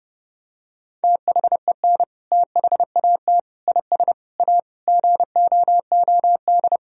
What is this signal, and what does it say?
Morse code, heard as a keyed tone, about 30 wpm, 700 Hz. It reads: THEN THAT IS A GOOD
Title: Sentences from Top 1000 Words - Review Part 4 - 30wpm